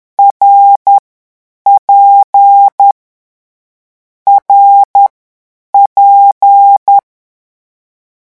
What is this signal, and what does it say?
Morse code, heard as a keyed tone, about 10 wpm, spaced as for 8 wpm, 800 Hz. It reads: RP RP
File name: morse mp3 / lecon6-vitesse 10